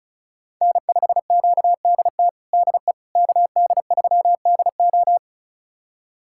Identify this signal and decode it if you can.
Morse code, heard as a keyed tone, about 34 wpm, 700 Hz. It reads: N5QDT DE KD3DO
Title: USA Callsign DE Callsign - 5 Characters - 35wpm